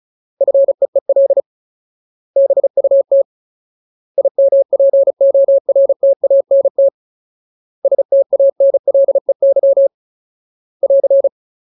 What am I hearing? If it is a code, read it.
Morse code, heard as a keyed tone, about 34 wpm, 550 Hz. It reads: FEEL BUT IMPORTANT STANLEY <AR>
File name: Ultimate Word List / 35WPM Ultimate